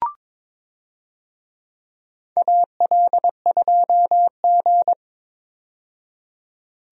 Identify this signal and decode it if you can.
Morse code, heard as a keyed tone, about 22 wpm, 700 Hz. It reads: AL2G
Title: USA Call Signs - 4 Characters - 22wpm